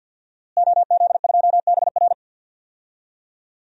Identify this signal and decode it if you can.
Morse code, heard as a keyed tone, about 51 wpm, 700 Hz. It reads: OZ2BR
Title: DXCC-100 Call Signs - 5 Characters - Repeated 3x - 50wpm